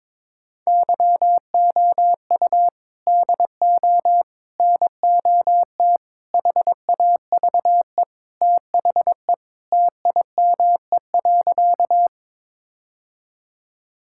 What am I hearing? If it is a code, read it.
Morse code, heard as a keyed tone, about 22 wpm, 700 Hz. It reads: YOU DO NOT HAVE THE TIME.
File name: practice / 22wpm / Sentences with 4-Word Phrases - 22wpm